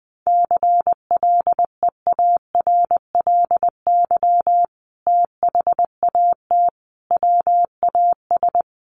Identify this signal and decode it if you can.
Morse code, heard as a keyed tone, about 20 wpm, 700 Hz. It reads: CLEARLY THAT WAS
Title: Sentences from Top 1300 Words 20wpm